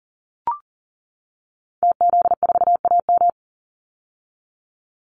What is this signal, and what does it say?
Morse code, heard as a keyed tone, about 40 wpm, 700 Hz. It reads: TZ4AM
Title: Call Signs - 40wpm